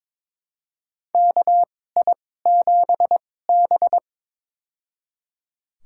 Morse code, heard as a keyed tone, about 22 wpm, spaced as for 16 wpm, 700 Hz. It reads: KI7B
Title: USA Call Signs - 4 Characters - 2x Character Spacing - 22wpm